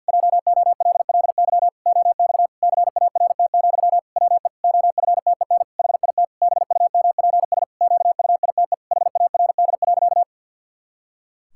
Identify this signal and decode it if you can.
Morse code, heard as a keyed tone, about 52 wpm, 700 Hz. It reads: JOLLY OX CART, WE OFTEN HIT BUMPS QUITE HARD.
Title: Pangram Challenge v2 - 50wpm